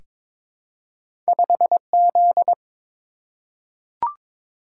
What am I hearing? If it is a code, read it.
Morse code, heard as a keyed tone, about 22 wpm, 700 Hz. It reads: 5Z
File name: practice / 22wpm / Two Letter-Number Combinations - 22wpm